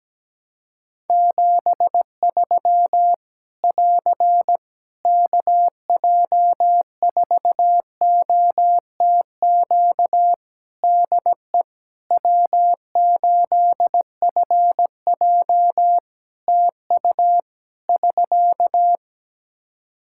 Morse code, heard as a keyed tone, about 17 wpm, 700 Hz. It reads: 73 <AR> KJ4OTQ DE W8FJ TU <SK>
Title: Practice QSO - Exchange 4 - Ending - 17wpm